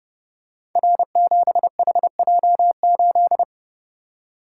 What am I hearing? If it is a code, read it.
Morse code, heard as a keyed tone, about 30 wpm, 700 Hz. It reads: R7HJ8
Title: Code Groups - 5 Letter:Numbers - 30wpm